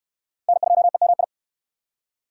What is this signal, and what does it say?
Morse code, heard as a keyed tone, about 69 wpm, 700 Hz. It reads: N2EMI